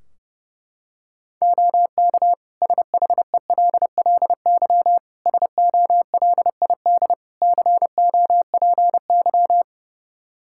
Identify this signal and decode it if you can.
Morse code, heard as a keyed tone, about 32 wpm, 700 Hz. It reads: OK SHELLY SOLID COPY
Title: Practice QSO - Exchange 3 - More Info - 30wpm